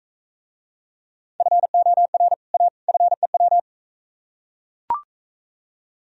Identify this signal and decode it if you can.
Morse code, heard as a keyed tone, about 43 wpm, 700 Hz. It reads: FOR A FEW